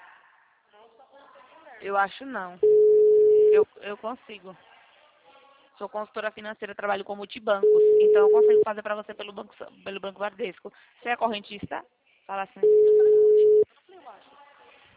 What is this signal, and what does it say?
Ringback tone (Europe, 425 Hz), with speech.
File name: out- -2010- - - .43257